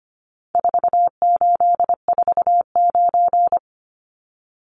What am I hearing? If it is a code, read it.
Morse code, heard as a keyed tone, about 25 wpm, 700 Hz. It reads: 4849
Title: Numbers - 4-Digit 25wpm